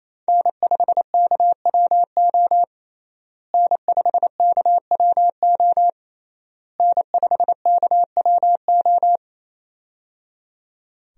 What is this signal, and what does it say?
Morse code, heard as a keyed tone, about 28 wpm, 700 Hz. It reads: N5KWO N5KWO N5KWO
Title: USA Call Signs - 5 Characters - Repeated 3x - 28wpm